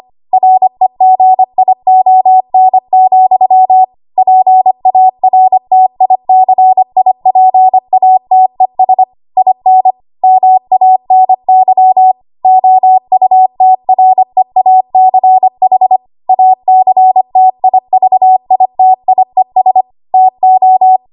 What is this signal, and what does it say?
Morse code, heard as a keyed tone, about 25 wpm, 750 Hz. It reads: REGION, PARTICIPATES IN MANY OUTREACH ACTIVITIES TO
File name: Morse / 25WPM